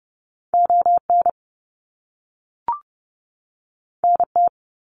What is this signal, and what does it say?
Morse code, heard as a keyed tone, about 30 wpm, 700 Hz. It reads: ON NT
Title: Canadian Provinces and Territories 30wpm